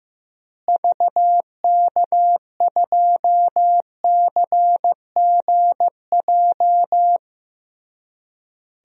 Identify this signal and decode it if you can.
Morse code, heard as a keyed tone, about 15 wpm, 700 Hz. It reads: VK2CGJ
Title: DXCC-100 Call Signs - 6 Characters - Repeated 3x - 15wpm